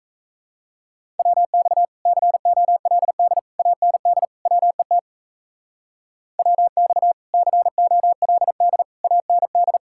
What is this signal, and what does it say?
Morse code, heard as a keyed tone, about 33 wpm, 700 Hz. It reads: WX COLD AND WET WX COLD AND
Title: Speed-Racing Common QSO Phrases - 28wpm